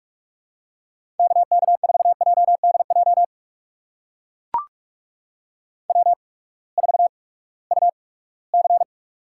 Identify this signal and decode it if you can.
Morse code, heard as a keyed tone, about 41 wpm, spaced as for 22 wpm, 700 Hz. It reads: KK4JDJ W4UC
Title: USA Call Signs - 8x Character Spacing - 45wpm